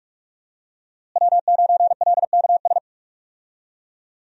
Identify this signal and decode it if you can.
Morse code, heard as a keyed tone, about 46 wpm, 700 Hz. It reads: W9LKS
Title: USA Call Signs - 5 Characters - Repeated 2x - 45wpm